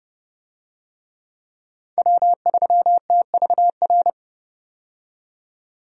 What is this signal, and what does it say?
Morse code, heard as a keyed tone, about 32 wpm, 700 Hz. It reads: W3TVR
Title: USA Call Signs - 5 Characters - 30wpm